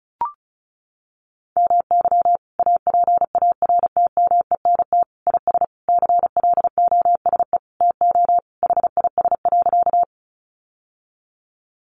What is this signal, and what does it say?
Morse code, heard as a keyed tone, about 34 wpm, 700 Hz. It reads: MY APARTMENT IS CLOSE TO HIS.
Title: Sentences from Top 900 Words 35wpm